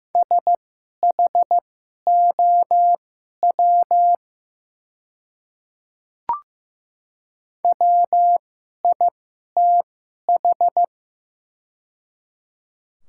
Morse code, heard as a keyed tone, about 15 wpm, spaced as for 11 wpm, 700 Hz. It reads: SHOW WITH
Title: 2x Character Spacing - Top 200 Words - 15wpm